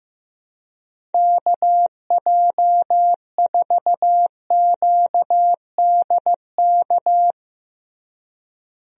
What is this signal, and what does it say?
Morse code, heard as a keyed tone, about 15 wpm, 700 Hz. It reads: KJ4QDK
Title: USA Call Signs - 6 Characters - Repeated 3x - 15wpm